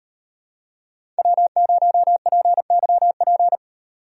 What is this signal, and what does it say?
Morse code, heard as a keyed tone, about 38 wpm, 700 Hz. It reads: W0PYP